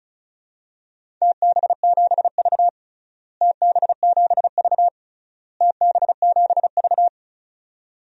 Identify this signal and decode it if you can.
Morse code, heard as a keyed tone, about 36 wpm, 700 Hz. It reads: TB7V TB7V TB7V
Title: DXCC-100 Call Signs - 4 Characters - Repeated 3x - 35wpm